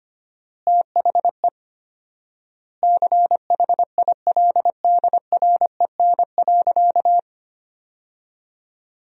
Morse code, heard as a keyed tone, about 25 wpm, 700 Hz. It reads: THE CHILDREN.
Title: 4x Word-Spacing - Sentences from Top 300 Words - 25wpm